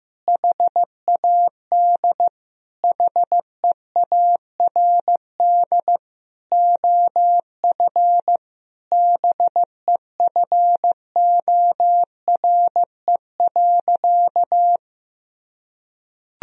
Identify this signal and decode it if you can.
Morse code, heard as a keyed tone, about 16 wpm, 700 Hz. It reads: HAD HEARD OF BEFORE.
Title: Sentences from Top 1000 Words - Review Part 3 - 15wpm